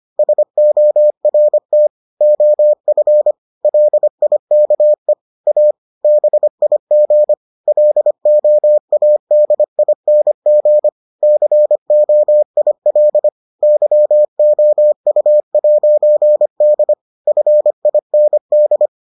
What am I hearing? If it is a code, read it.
Morse code, heard as a keyed tone, about 25 wpm, 600 Hz. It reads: SORT OF LIKE A BIG LOADING COIL YOU'D FIND